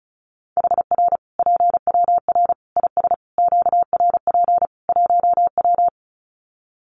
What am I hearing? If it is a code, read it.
Morse code, heard as a keyed tone, about 34 wpm, 700 Hz. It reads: HR PWR IS QRP 1W
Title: Common QSO Phrases 35wpm